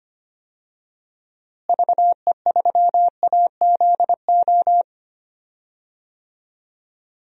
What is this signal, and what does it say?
Morse code, heard as a keyed tone, about 25 wpm, 700 Hz. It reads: VE3AZO
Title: DXCC-100 Call Signs - 6 Characters - Repeated 2x - 25wpm